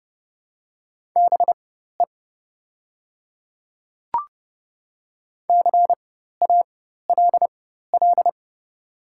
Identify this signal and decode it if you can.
Morse code, heard as a keyed tone, about 30 wpm, spaced as for 14 wpm, 700 Hz. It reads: BE CALL
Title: 4x Character Spacing - Top 100 Words - 30wpm